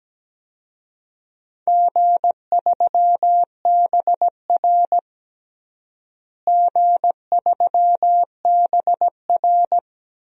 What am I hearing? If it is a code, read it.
Morse code, heard as a keyed tone, about 17 wpm, 700 Hz. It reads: G3BR G3BR
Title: DXCC-100 Call Signs - 4 Characters - Repeated 2x - 17wpm